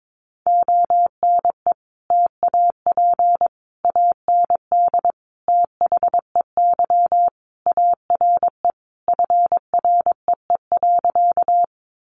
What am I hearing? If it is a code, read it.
Morse code, heard as a keyed tone, about 22 wpm, 700 Hz. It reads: ONE TAP AND THEY ARE FREE.
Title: Sentences from Top 1400 Words 22wpm